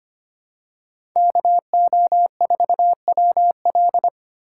Morse code, heard as a keyed tone, about 25 wpm, 700 Hz. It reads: KO4WL
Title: USA Call Signs - 5 Characters - 2x Character Spacing - 25wpm